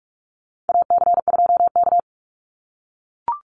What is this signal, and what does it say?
Morse code, heard as a keyed tone, about 44 wpm, 700 Hz. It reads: AC2K
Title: Call Signs - 45wpm